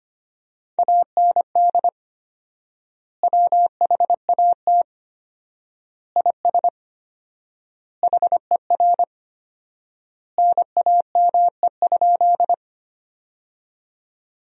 Morse code, heard as a keyed tone, about 25 wpm, 700 Hz. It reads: AND WHAT IS HER NAME?
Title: 4x Word-Spacing - Sentences from Top 200 Words - 25wpm